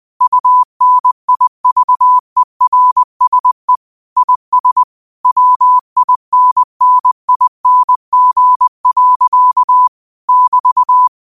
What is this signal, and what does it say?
Morse code, heard as a keyed tone, about 20 wpm, 1000 Hz. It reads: UNIVERSE IS WINNING. <BT>
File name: qotdfiles / 20wpm